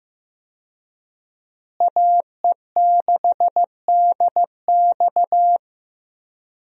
Morse code, heard as a keyed tone, about 15 wpm, 700 Hz. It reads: AE6DX
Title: USA Call Signs - 5 Characters - Repeated 3x - 15wpm